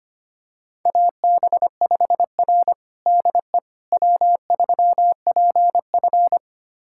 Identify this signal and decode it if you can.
Morse code, heard as a keyed tone, about 25 wpm, 700 Hz. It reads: AB5R DE W3PF
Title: USA Callsign DE Callsign - 4 Characters - 25wpm